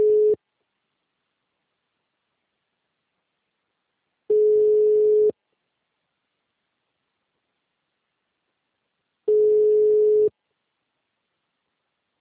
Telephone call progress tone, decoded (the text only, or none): ringback tone